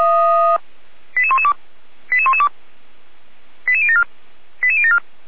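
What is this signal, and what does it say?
Five-tone selective call (ZVEI), heard as quick runs of five tones, tones of about 70 ms. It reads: 88112 88112 88074 88074
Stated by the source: Five-tone ZVEI selective call received by SDR, logged at audio.f85360.feuerwehr-orlamuende.org (Feuerwehr Orlamünde fire service).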